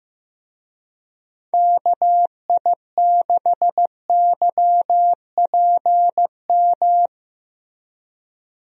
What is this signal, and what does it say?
Morse code, heard as a keyed tone, about 15 wpm, 700 Hz. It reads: KI6YPM
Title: USA Call Signs - 6 Characters - Repeated 3x - 15wpm